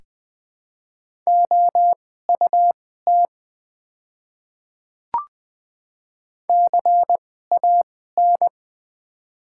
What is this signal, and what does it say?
Morse code, heard as a keyed tone, about 21 wpm, spaced as for 15 wpm, 700 Hz. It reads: OUT CAN
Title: 2x Character Spacing - Top 100 Words - 20wpm